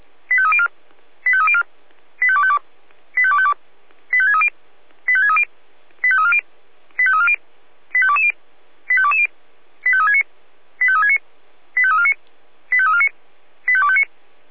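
Five-tone selective call (ZVEI), heard as quick runs of five tones, tones of about 70 ms. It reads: 85304 85304 85252 85252 85630 85630 85340 85340 85220 85220 85369 85369 85349 85349 85259
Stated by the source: SDR-received ZVEI five-tone selective call, code 85275.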